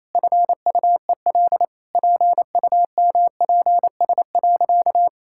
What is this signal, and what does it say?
Morse code, heard as a keyed tone, about 28 wpm, 700 Hz. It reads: FUEL PUMPS.
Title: Pangram Challenge v2 - 28wpm